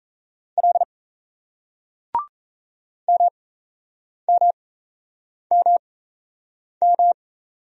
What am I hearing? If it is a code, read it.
Morse code, heard as a keyed tone, about 38 wpm, spaced as for 8 wpm, 700 Hz. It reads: L MMMM